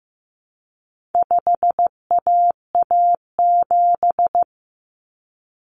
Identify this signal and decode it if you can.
Morse code, heard as a keyed tone, about 15 wpm, 700 Hz. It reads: 5AA7
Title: Common Cut Numbers - 4-Digit 15wpm